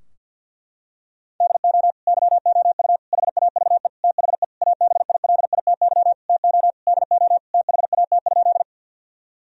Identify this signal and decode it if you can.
Morse code, heard as a keyed tone, about 50 wpm, 700 Hz. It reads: DO YOU HAVE THE ABILITY TO DO THAT?
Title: Sentences from Top 1000 Words - Review Part 3 - 50wpm